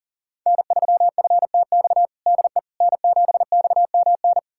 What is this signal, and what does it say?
Morse code, heard as a keyed tone, about 40 wpm, 700 Hz. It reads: N3FTX DE N7XMN
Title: USA Callsign DE Callsign - 5 Characters - 40wpm